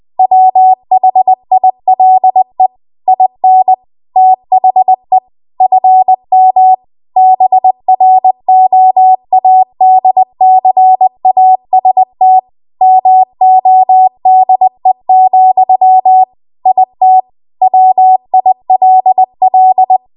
Morse code, heard as a keyed tone, about 20 wpm, 750 Hz. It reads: WHILE IN THE FM BROADCAST MODE, IT WILL
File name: Archive / 20 WPM / 20WPM